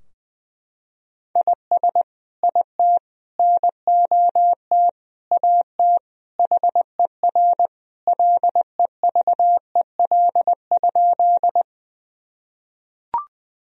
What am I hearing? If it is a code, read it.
Morse code, heard as a keyed tone, about 20 wpm, 700 Hz. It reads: IS IT NOT AT HER LEVEL?
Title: Sentences from Top 1000 Words - Review Part 4 - 20wpm